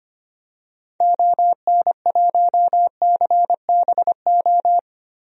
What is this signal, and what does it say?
Morse code, heard as a keyed tone, about 25 wpm, 700 Hz. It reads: ON1CBO
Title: DXCC-100 Call Signs - 6 Characters - 25wpm